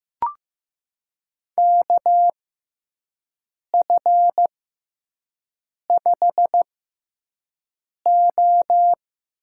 Morse code, heard as a keyed tone, about 15 wpm, spaced as for 5 wpm, 700 Hz. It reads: KF5O
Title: USA Call Signs - 6 Characters - 6x Character Spacing - 15wpm